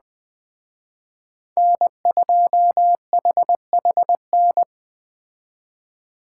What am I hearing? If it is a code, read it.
Morse code, heard as a keyed tone, about 20 wpm, 700 Hz. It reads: N2HHN